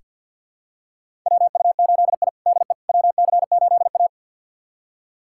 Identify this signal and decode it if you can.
Morse code, heard as a keyed tone, about 48 wpm, 700 Hz. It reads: WU8I DE WC8A
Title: USA Callsign DE Callsign - 4 Characters - 50wpm